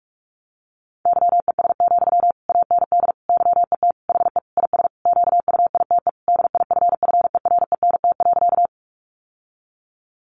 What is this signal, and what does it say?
Morse code, heard as a keyed tone, about 44 wpm, 700 Hz. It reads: YES, AND YET HE IS QUITE DIFFERENT.
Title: Sentences from Top 700 Words 45wpm